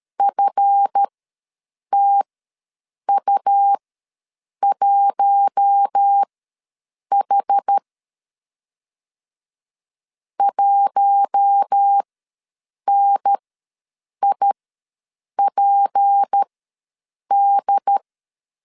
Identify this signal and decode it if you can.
Morse code, heard as a keyed tone, about 13 wpm, spaced as for 7 wpm, 800 Hz. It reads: FTU1H 1NIPD
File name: LEZIONI / 7 GF9Z / 7 GF ws 12wpm 3140